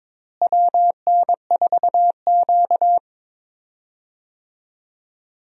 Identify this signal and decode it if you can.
Morse code, heard as a keyed tone, about 22 wpm, 700 Hz. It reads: WN4Q
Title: USA Call Signs - 4 Characters - Repeated 3x - 22wpm